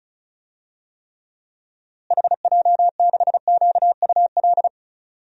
Morse code, heard as a keyed tone, about 36 wpm, 700 Hz. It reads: HJ6QUL
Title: DXCC-100 Call Signs - 6 Characters - Repeated 3x - 35wpm